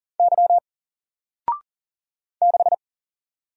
Morse code, heard as a keyed tone, about 40 wpm, 700 Hz. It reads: Y 6
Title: Single Letter-Number Alt 40wpm